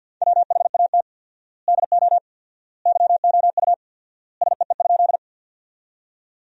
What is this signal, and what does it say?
Morse code, heard as a keyed tone, about 50 wpm, 700 Hz. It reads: WHAT DO YOU SEE?
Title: 4x Word-Spacing - Sentences from Top 100 Words - 50wpm